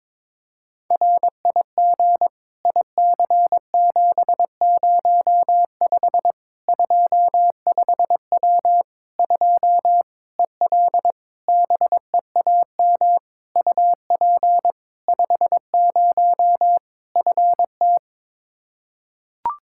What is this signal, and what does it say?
Morse code, heard as a keyed tone, about 22 wpm, 700 Hz. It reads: RIG IC705 25W 2 EL BEAM UP 50 FT
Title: Practice QSO - Exchange 2 - Info - 22wpm